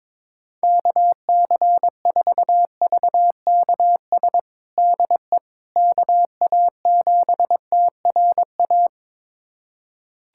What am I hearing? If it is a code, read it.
Morse code, heard as a keyed tone, about 22 wpm, 700 Hz. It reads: KC4VKS DE KA7TRA
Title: USA Callsign DE Callsign 22wpm